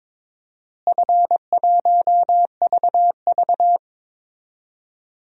Morse code, heard as a keyed tone, about 22 wpm, 700 Hz. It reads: F1VV
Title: DXCC-100 Call Signs - 4 Characters - Repeated 3x - 22wpm